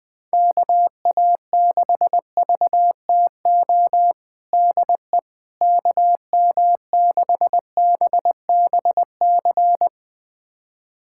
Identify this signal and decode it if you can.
Morse code, heard as a keyed tone, about 20 wpm, 700 Hz. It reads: KA6VTO DE KM6BBC
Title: USA Callsign DE Callsign 20wpm